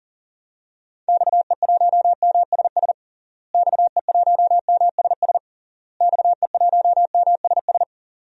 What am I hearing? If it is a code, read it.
Morse code, heard as a keyed tone, about 40 wpm, 700 Hz. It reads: XE1MSS XE1MSS XE1MSS
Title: DXCC-100 Call Signs - 6 Characters - Repeated 3x - 40wpm